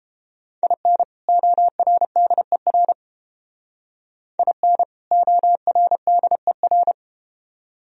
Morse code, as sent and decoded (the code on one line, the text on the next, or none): .. -. / --- .-. -.. .-. / .. -. / --- .-. -.. .-.
IN ORDER IN ORDER